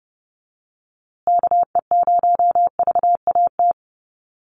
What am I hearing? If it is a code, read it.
Morse code, heard as a keyed tone, about 30 wpm, 700 Hz. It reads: KE0VAT
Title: USA Call Signs Repeated 2x 30wpm